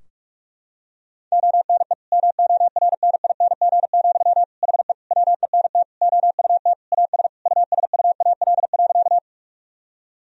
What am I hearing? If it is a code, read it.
Morse code, heard as a keyed tone, about 43 wpm, 700 Hz. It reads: ONE MORNING, HE WENT OUT AS USUAL.
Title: Sentences from Top 1000 Words - Review Part 2 - 45wpm